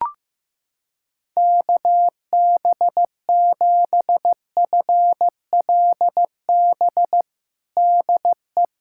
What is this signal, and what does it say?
Morse code, heard as a keyed tone, about 15 wpm, 700 Hz. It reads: KB7FLB DE
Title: Practice QSO - Exchange 4 - Ending - 15wpm